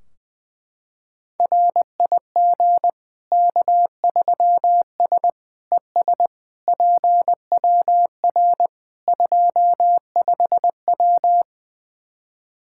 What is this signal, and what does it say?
Morse code, heard as a keyed tone, about 20 wpm, 700 Hz. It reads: RIG K3S ES PWR 25W
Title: Practice QSO - Exchange 2 - Info - 20wpm